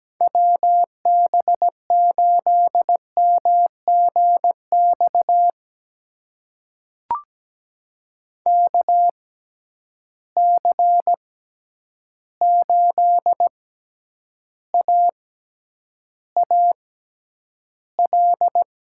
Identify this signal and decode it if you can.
Morse code, heard as a keyed tone, about 17 wpm, spaced as for 9 wpm, 700 Hz. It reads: WB8MGX KC8AAL
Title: USA Call Signs - 6 Characters - 6x Character Spacing - 17wpm